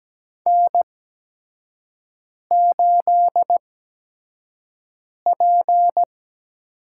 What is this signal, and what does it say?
Morse code, heard as a keyed tone, about 17 wpm, spaced as for 5 wpm, 700 Hz. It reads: N8P
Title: USA Call Signs - 5 Characters - 8x Character Spacing - 17wpm